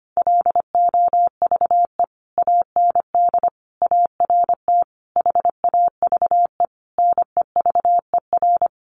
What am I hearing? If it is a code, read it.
Morse code, heard as a keyed tone, about 25 wpm, 700 Hz. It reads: LOVE AND ART HAVE NEVER
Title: Sentences from Top 1100 Words 25wpm